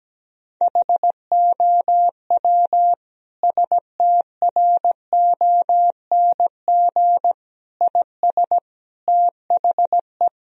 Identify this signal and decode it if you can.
Morse code, heard as a keyed tone, about 17 wpm, 700 Hz. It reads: HOW STRONG IS THE
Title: Sentences from Top 1000 Words - Review Part 4 - 17wpm